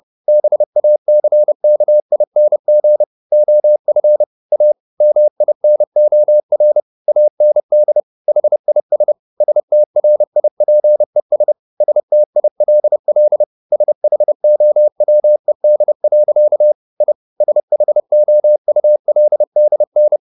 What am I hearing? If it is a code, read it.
Morse code, heard as a keyed tone, about 30 wpm, 600 Hz. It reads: BACKING OF A MINOR AND HIS STRIPES STILL SHOWED. I SHOULDN